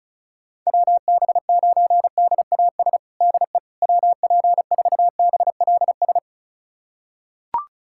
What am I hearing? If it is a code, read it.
Morse code, heard as a keyed tone, about 36 wpm, 700 Hz. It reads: WB9DAS DE WP4BLS
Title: USA Callsign DE Callsign - 6 Characters - 35wpm